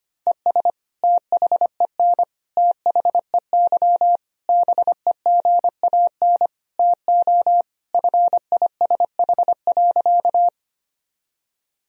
Morse code, heard as a keyed tone, about 25 wpm, 700 Hz. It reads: ES THEN THEY BEGAN TO FISH.
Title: Sentences from Top 300 Words and Abbreviations 25wpm